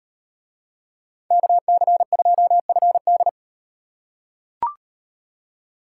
Morse code, heard as a keyed tone, about 38 wpm, 700 Hz. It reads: KC2FD